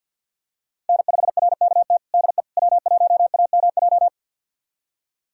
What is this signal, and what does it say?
Morse code, heard as a keyed tone, about 50 wpm, 700 Hz. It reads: N5RKT DE W1AMJ
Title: USA Callsign DE Callsign - 5 Characters - 50wpm